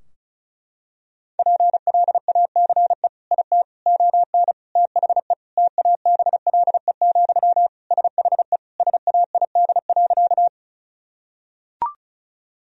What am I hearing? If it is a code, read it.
Morse code, heard as a keyed tone, about 34 wpm, 700 Hz. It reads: PLACE IT ON THE TABLE, SHE SAID.
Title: Sentences from Top 1000 Words - Review Part 2 - 35wpm